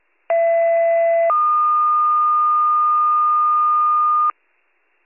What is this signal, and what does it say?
Two-tone page, heard as a one-second tone, then a three-second tone, about 670 then 1220 Hz. It is heard from 0.3 to 4.3 s.